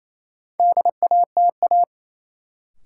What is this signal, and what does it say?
Morse code, heard as a keyed tone, about 28 wpm, 700 Hz. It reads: DATA
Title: Sentences from Top 1000 Words - Review Part 3 - New Word Drill - 28wpm